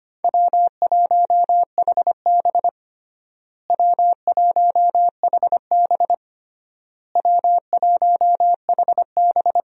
Morse code, heard as a keyed tone, about 25 wpm, 700 Hz. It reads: W1HB W1HB W1HB
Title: USA Call Signs - 4 Characters - Repeated 3x - 25wpm